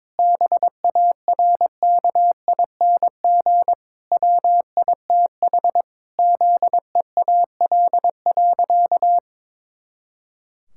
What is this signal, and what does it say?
Morse code, heard as a keyed tone, about 22 wpm, 700 Hz. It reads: BARKING WITH ZEAL.
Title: Pangram Challenge v2 - 22wpm